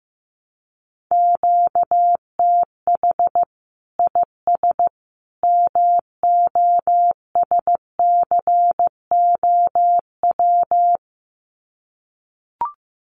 Morse code, heard as a keyed tone, about 15 wpm, 700 Hz. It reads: QTH IS MOSCOW
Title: Common QSO Phrases 15wpm